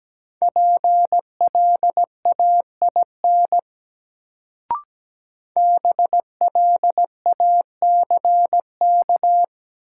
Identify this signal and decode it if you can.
Morse code, heard as a keyed tone, about 17 wpm, 700 Hz. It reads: PLAIN BLACK
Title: Sentences from Top 400 Words - New Word Drill - 17wpm